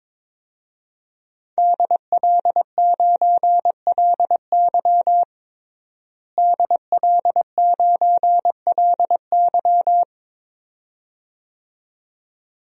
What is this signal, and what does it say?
Morse code, heard as a keyed tone, about 22 wpm, 700 Hz. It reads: DL9LY DL9LY
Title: DXCC-100 Call Signs - 5 Characters - Repeated 2x - 22wpm